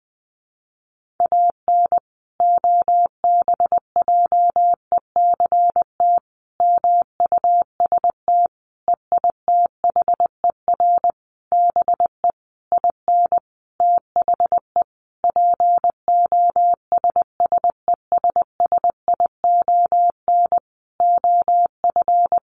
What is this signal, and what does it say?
Morse code, heard as a keyed tone, about 20 wpm, 700 Hz. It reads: AN OBJECT MUST EITHER BE IN THE POSSESSION OF
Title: Sentences from Top 1400 Words 20wpm